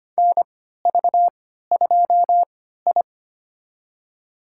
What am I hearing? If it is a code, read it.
Morse code, heard as a keyed tone, about 25 wpm, spaced as for 14 wpm, 700 Hz. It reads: NV2I
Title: USA Call Signs - 3x Character Spacing - 25wpm